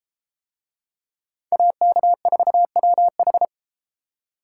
Morse code, heard as a keyed tone, about 30 wpm, 700 Hz. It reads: AK4WH